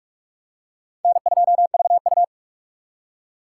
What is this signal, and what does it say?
Morse code, heard as a keyed tone, about 44 wpm, 700 Hz. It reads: N2VU